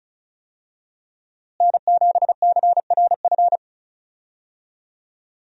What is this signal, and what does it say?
Morse code, heard as a keyed tone, about 36 wpm, 700 Hz. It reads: N7CRF
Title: USA Call Signs - 5 Characters - Repeated 3x - 35wpm